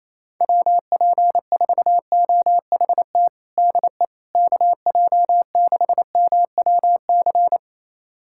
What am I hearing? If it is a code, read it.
Morse code, heard as a keyed tone, about 28 wpm, 700 Hz. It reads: WP4OHT DE KJ6MWC
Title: USA Callsign DE Callsign 28wpm